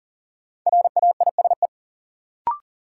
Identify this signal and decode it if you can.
Morse code, heard as a keyed tone, about 38 wpm, 700 Hz. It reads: RAISE
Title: Sentences from Top 1400 Words - New Word Drill - 40wpm